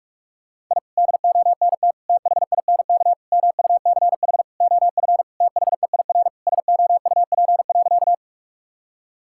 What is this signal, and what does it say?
Morse code, heard as a keyed tone, about 46 wpm, 700 Hz. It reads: I DONT THINK MUCH OF THEIR SOUP.
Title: Sentences from Top 1000 Words - Review Part 6 - 45wpm